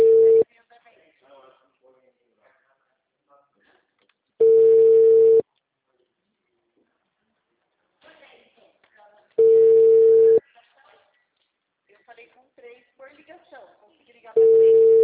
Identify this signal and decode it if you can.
Ringback tone (China, 450 Hz), with speech.
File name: out- -2006- - - .12264